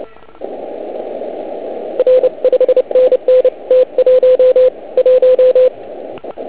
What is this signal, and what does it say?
Morse code, heard as a keyed tone, about 30 wpm, spaced as for 26 wpm, 510 Hz. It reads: R 5NN T11